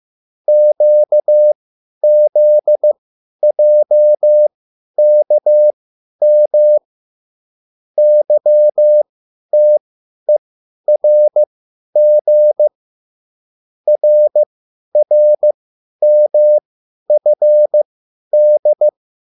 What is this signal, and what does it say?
Morse code, heard as a keyed tone, about 15 wpm, spaced as for 10 wpm, 600 Hz. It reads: QZJKM YTERG RRMFD